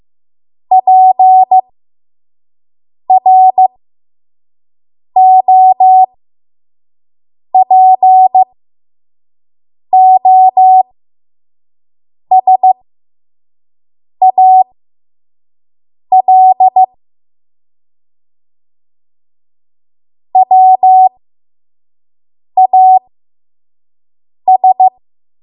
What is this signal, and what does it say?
Morse code, heard as a keyed tone, about 15 wpm, spaced as for 5 wpm, 750 Hz. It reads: PROPOSAL WAS
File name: Morse / 05WPM